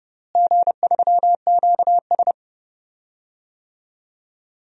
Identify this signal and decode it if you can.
Morse code, heard as a keyed tone, about 30 wpm, 700 Hz. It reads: G3QS
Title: DXCC-100 Call Signs - 4 Characters - Repeated 2x - 30wpm